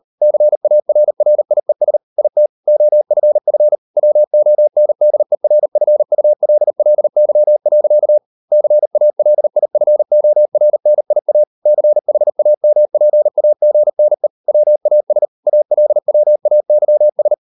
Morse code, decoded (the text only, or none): CARRIES IT OFF WONDERFULLY. CALIFORNIA CHAMPAGNE WAS ALWAYS